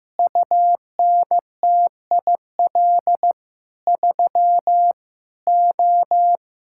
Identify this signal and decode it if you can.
Morse code, heard as a keyed tone, about 16 wpm, 700 Hz. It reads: UNTIL 3 O
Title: Pangram Challenge v2 - 15wpm